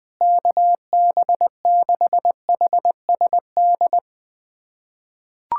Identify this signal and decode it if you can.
Morse code, heard as a keyed tone, about 21 wpm, 700 Hz. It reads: KB6HSD
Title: USA Callsign DE Callsign 20wpm